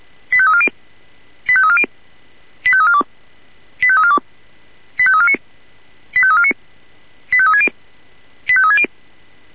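Five-tone selective call (ZVEI), heard as quick runs of five tones, tones of about 70 ms. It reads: 85340 85340 85342 85342 85349 85349 85360 85360
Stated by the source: Five-tone ZVEI selective call received on SDR.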